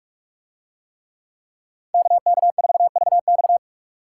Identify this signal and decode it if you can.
Morse code, heard as a keyed tone, about 41 wpm, 700 Hz. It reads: KK4VX